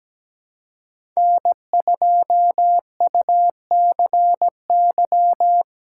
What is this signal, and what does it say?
Morse code, heard as a keyed tone, about 17 wpm, 700 Hz. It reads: N2UCY
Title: USA Call Signs - 5 Characters - 2x Character Spacing - 17wpm